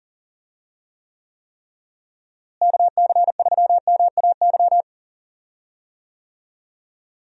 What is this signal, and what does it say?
Morse code, heard as a keyed tone, about 40 wpm, 700 Hz. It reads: KC3MAY
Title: USA Call Signs - 6 Characters - Repeated 3x - 40wpm